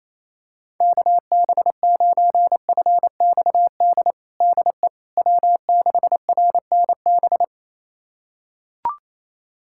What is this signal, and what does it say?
Morse code, heard as a keyed tone, about 28 wpm, 700 Hz. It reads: KB9FXD DE W6RNB
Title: USA Callsign DE Callsign 28wpm